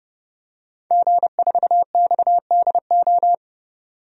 Morse code, heard as a keyed tone, about 30 wpm, 700 Hz. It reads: G4XDO
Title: DXCC-100 Call Signs - 5 Characters - 30wpm